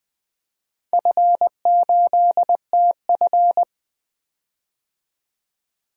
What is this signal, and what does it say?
Morse code, heard as a keyed tone, about 20 wpm, 700 Hz. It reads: F8TF
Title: DXCC-100 Call Signs - 4 Characters - 20wpm